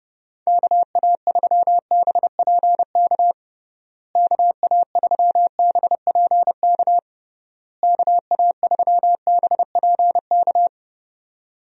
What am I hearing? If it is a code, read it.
Morse code, heard as a keyed tone, about 30 wpm, 700 Hz. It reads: KA3BPK KA3BPK KA3BPK
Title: USA Call Signs - 6 Characters - Repeated 3x - 30wpm